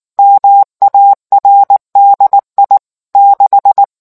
Morse code, heard as a keyed tone, about 19 wpm, 800 Hz. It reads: MARDI 6